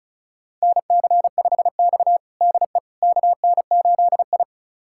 Morse code, heard as a keyed tone, about 34 wpm, 700 Hz. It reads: NC5X DE KN8I
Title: USA Callsign DE Callsign - 4 Characters - 35wpm